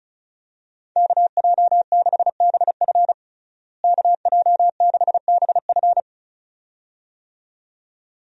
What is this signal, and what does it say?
Morse code, heard as a keyed tone, about 35 wpm, 700 Hz. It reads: KJ6BF KJ6BF
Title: USA Call Signs - 5 Characters - Repeated 2x - 35wpm